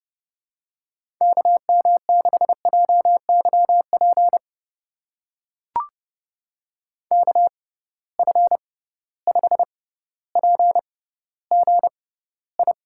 Morse code, heard as a keyed tone, about 30 wpm, spaced as for 15 wpm, 700 Hz. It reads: KM6JYP KF5PGI
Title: USA Call Signs - 6 Characters - 6x Character Spacing - 30wpm